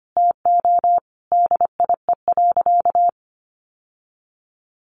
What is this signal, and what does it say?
Morse code, heard as a keyed tone, about 25 wpm, 700 Hz. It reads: TO DIE.
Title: Sentences from Top 1300 Words 25wpm